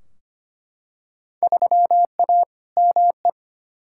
Morse code, heard as a keyed tone, about 25 wpm, 700 Hz. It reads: 3A ME
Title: Field Day Exchanges - 25wpm